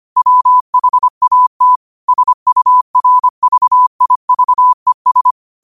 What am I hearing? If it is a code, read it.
Morse code, heard as a keyed tone, about 25 wpm, 1000 Hz. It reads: WHAT SURVIVES